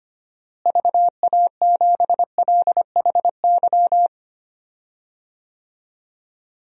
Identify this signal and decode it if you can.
Morse code, heard as a keyed tone, about 25 wpm, 700 Hz. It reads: VA7LHY
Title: DXCC-100 Call Signs - 6 Characters - Repeated 3x - 25wpm